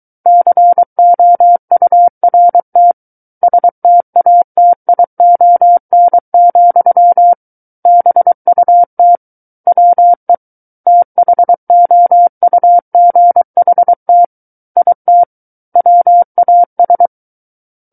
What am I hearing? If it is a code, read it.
Morse code, heard as a keyed tone, about 23 wpm, 700 Hz. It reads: COURT STATION, BUT WE THOUGHT IT WAS
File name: Book One / 14-Chapter 14 - 23 WPM